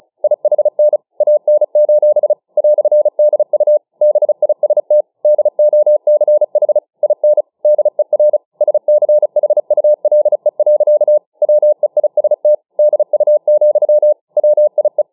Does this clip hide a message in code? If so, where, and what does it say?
Yes: Morse code from 0.2 to 15.0 s, heard as a keyed tone, about 35 wpm, 600 Hz. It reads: IHN AN: "DU BIST DOCH IN DER SCHULE. WEIST DU, WIE